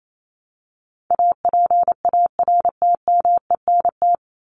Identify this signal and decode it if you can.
Morse code, heard as a keyed tone, about 28 wpm, 700 Hz. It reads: APARTMENT
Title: Top 751-1000 Words 28wpm